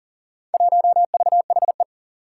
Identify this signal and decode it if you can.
Morse code, heard as a keyed tone, about 40 wpm, 700 Hz. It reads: 1VHE
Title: Four Challenging Letter-Number Combinations 40wpm